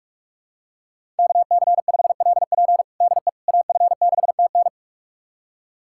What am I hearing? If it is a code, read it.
Morse code, heard as a keyed tone, about 46 wpm, 700 Hz. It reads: KC5LP DE AF6TN
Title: USA Callsign DE Callsign - 5 Characters - 45wpm